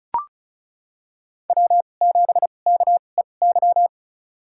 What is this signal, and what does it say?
Morse code, heard as a keyed tone, about 36 wpm, spaced as for 25 wpm, 700 Hz. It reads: W7KEY
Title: USA Call Signs - 5 Characters - 2x Character Spacing - 35wpm